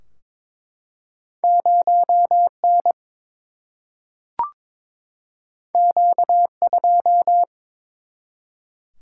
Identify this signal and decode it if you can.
Morse code, heard as a keyed tone, about 22 wpm, 700 Hz. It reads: 0N Q2
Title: Single-Letter Single-Number 22wpm